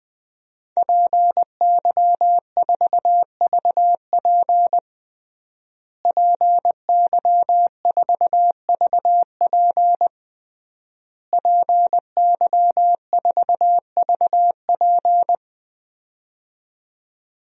Morse code, heard as a keyed tone, about 20 wpm, 700 Hz. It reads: PY4VP PY4VP PY4VP
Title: DXCC-100 Call Signs - 5 Characters - Repeated 3x - 20wpm